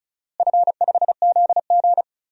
Morse code, heard as a keyed tone, about 36 wpm, 700 Hz. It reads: F5ZG
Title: DXCC-100 Call Signs - 4 Characters - Repeated 2x - 35wpm